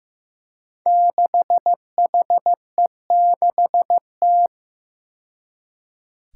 Morse code, heard as a keyed tone, about 15 wpm, 700 Hz. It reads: 6HE6T
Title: Five Challenging Letter-Number Combinations 15wpm